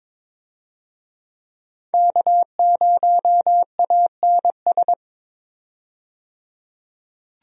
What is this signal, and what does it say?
Morse code, heard as a keyed tone, about 22 wpm, 700 Hz. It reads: K0ANS
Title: USA Call Signs - 5 Characters - 22wpm